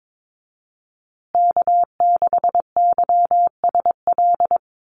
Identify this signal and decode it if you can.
Morse code, heard as a keyed tone, about 22 wpm, 700 Hz. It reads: K6YSL
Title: USA Call Signs Repeated 2x 22wpm